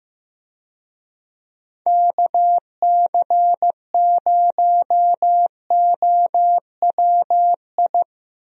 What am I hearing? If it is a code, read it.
Morse code, heard as a keyed tone, about 15 wpm, 700 Hz. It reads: KC0OWI
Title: USA Call Signs - 6 Characters - Repeated 3x - 15wpm